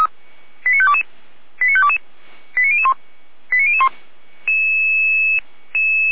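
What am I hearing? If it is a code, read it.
Five-tone selective call (ZVEI), heard as quick runs of five tones, tones of about 70 ms. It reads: 89522 89522 89001 89001
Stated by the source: Received by SDR; ZVEI five-tone call 89273, channel 455.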